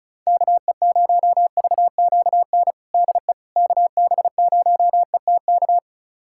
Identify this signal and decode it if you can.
Morse code, heard as a keyed tone, about 36 wpm, 700 Hz. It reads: KE0VQN DE KB0ETK
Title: USA Callsign DE Callsign 35wpm